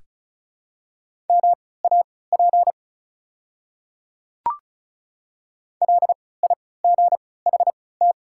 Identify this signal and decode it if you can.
Morse code, heard as a keyed tone, about 36 wpm, spaced as for 20 wpm, 690 Hz. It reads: MAP LIGHT
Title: 3x Character Spacing - Top 500 Words - 35wpm